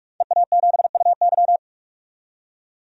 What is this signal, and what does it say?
Morse code, heard as a keyed tone, about 45 wpm, 700 Hz. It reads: EA7UY
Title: DXCC-100 Call Signs - 5 Characters - 45wpm